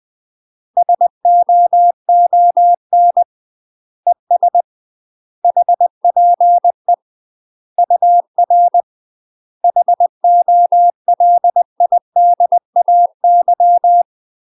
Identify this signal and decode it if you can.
Morse code, heard as a keyed tone, about 20 wpm, 700 Hz. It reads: SOON ES HPE UR HOLIDAY
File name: UK MASTER QSOs TRAINING/20WPM